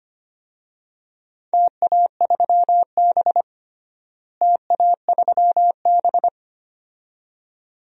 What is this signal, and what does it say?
Morse code, heard as a keyed tone, about 25 wpm, 700 Hz. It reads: TA3B TA3B
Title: DXCC-100 Call Signs - 4 Characters - Repeated 2x - 25wpm